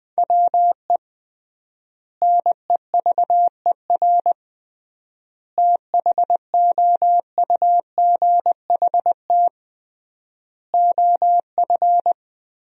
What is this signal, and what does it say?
Morse code, heard as a keyed tone, about 20 wpm, 700 Hz. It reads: WE NEVER THOUGHT OF
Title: 3x Word-Spacing - Sentences from Top 300 Words - 20wpm